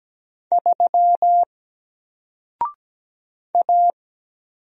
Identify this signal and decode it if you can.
Morse code, heard as a keyed tone, about 17 wpm, 700 Hz. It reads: 3 A
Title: Single Letter-Number 17wpm